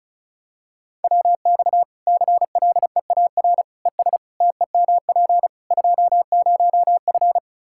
Morse code, heard as a keyed tone, about 36 wpm, 700 Hz. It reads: WX CLEAR ES TEMP 20F
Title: K1USN SST QSO Practice 35wpm